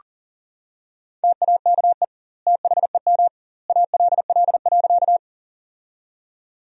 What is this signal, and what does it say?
Morse code, heard as a keyed tone, about 40 wpm, 700 Hz. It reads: TAKE THEM ALL.
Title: 2x Word-Spacing - Sentences from Top 200 Words - 40wpm